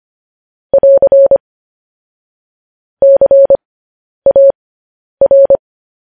Morse code, heard as a keyed tone, about 25 wpm, spaced as for 10 wpm, 550 Hz. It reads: <AR> CAR
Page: news-headlines-in-morse-code-at-10-wpm